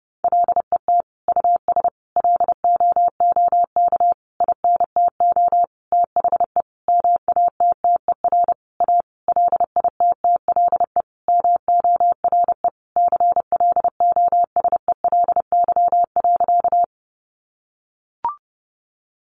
Morse code, heard as a keyed tone, about 30 wpm, 700 Hz. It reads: LET US LOOK INTO THE MATTER A LITTLE MORE CLOSELY.
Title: Sentences from Top 1400 Words 30wpm